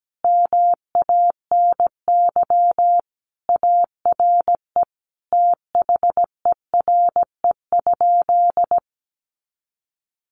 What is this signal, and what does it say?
Morse code, heard as a keyed tone, about 17 wpm, 700 Hz. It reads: MANY ARE THERE?
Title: Sentences from Top 100 Words 17wpm